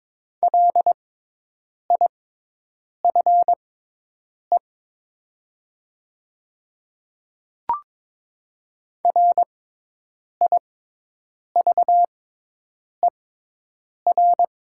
Morse code, heard as a keyed tone, about 22 wpm, spaced as for 8 wpm, 700 Hz. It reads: LIFE RIVER
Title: 6x Character Spacing - Top 500 Words - 22wpm